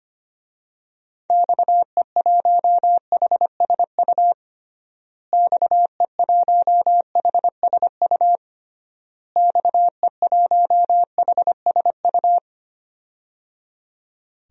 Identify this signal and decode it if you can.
Morse code, heard as a keyed tone, about 25 wpm, 700 Hz. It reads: XE1HSU XE1HSU XE1HSU
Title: DXCC-100 Call Signs - 6 Characters - Repeated 3x - 25wpm